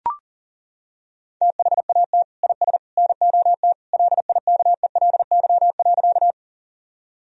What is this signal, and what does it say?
Morse code, heard as a keyed tone, about 40 wpm, 700 Hz. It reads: THAT IS NOT LIKELY.
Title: Sentences from Top 1000 Words - Review Part 4 - 40wpm